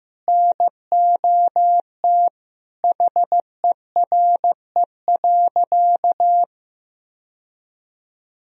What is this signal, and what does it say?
Morse code, heard as a keyed tone, about 15 wpm, 700 Hz. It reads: NOT HERE.
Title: Sentences from Top 1000 Words - Review Part 1 - 15wpm